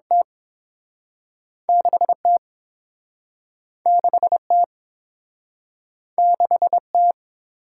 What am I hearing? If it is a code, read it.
Morse code, heard as a keyed tone, about 30 wpm, spaced as for 6 wpm, 700 Hz. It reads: T6 T6 T6 T